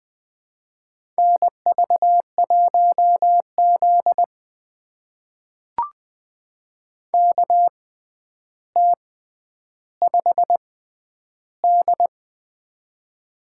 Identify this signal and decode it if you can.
Morse code, heard as a keyed tone, about 20 wpm, spaced as for 10 wpm, 700 Hz. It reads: NV1Z KT5D
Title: USA Call Signs - 4 Characters - 6x Character Spacing - 20wpm